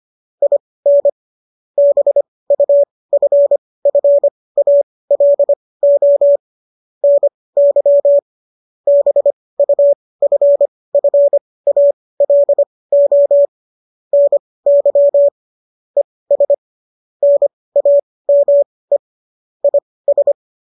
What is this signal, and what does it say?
Morse code, heard as a keyed tone, about 25 wpm, spaced as for 18 wpm, 570 Hz. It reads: IN BUFFALO NY BUFFALO NY ES NAME IS
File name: qso223 18